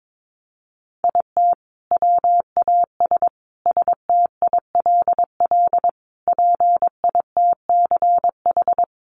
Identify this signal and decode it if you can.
Morse code, heard as a keyed tone, about 22 wpm, 700 Hz. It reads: IT WAS STILL PITCH
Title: Sentences from Top 1300 Words 22wpm